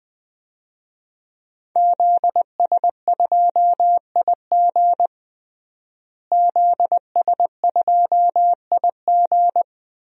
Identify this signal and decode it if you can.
Morse code, heard as a keyed tone, about 21 wpm, 700 Hz. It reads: ZS2IG ZS2IG
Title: DXCC-100 Call Signs - 5 Characters - Repeated 3x - 20wpm